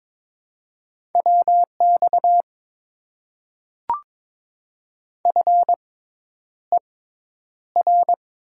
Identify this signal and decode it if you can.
Morse code, heard as a keyed tone, about 22 wpm, spaced as for 10 wpm, 700 Hz. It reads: WX FER
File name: Top 90 QSO Elements - 6x Character Spacing - 22wpm